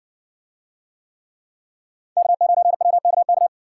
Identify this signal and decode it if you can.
Morse code, heard as a keyed tone, about 60 wpm, 700 Hz. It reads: K9WKK